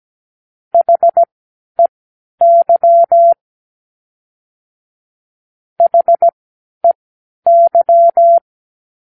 Morse code, heard as a keyed tone, about 17 wpm, spaced as for 11 wpm, 680 Hz. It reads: HEY HEY
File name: k7qo / Practicewords